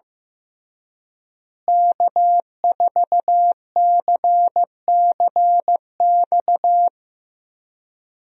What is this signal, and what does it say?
Morse code, heard as a keyed tone, about 15 wpm, 700 Hz. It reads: K4CCX
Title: USA Call Signs - 5 Characters - Repeated 3x - 15wpm